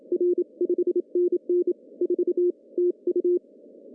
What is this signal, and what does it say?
Morse code, heard as a keyed tone, about 28 wpm, 350 Hz. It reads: R 5NN 4 TU
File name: CW / cqww-cw-k5tr